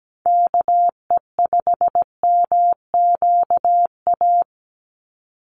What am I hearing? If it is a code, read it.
Morse code, heard as a keyed tone, about 17 wpm, 700 Hz. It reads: KE5MQA
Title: USA Call Signs 17wpm